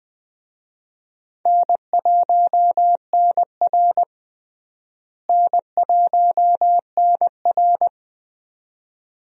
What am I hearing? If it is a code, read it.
Morse code, heard as a keyed tone, about 20 wpm, 700 Hz. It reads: N1NR N1NR
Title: USA Call Signs - 4 Characters - Repeated 2x - 20wpm